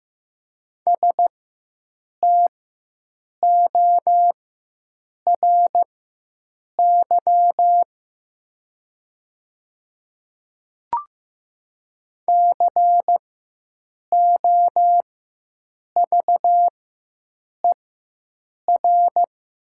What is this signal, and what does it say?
Morse code, heard as a keyed tone, about 15 wpm, spaced as for 7 wpm, 700 Hz. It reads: STORY COVER
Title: 4x Character Spacing - Top 300 Words - 15wpm